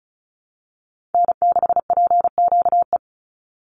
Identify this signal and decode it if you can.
Morse code, heard as a keyed tone, about 34 wpm, 700 Hz. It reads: N6PQE
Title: USA Call Signs 35wpm